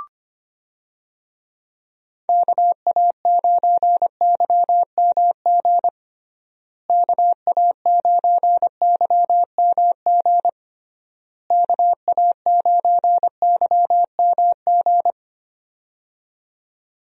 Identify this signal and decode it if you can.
Morse code, heard as a keyed tone, about 25 wpm, 700 Hz. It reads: KA9YMG KA9YMG KA9YMG
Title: USA Call Signs - 6 Characters - Repeated 3x - 25wpm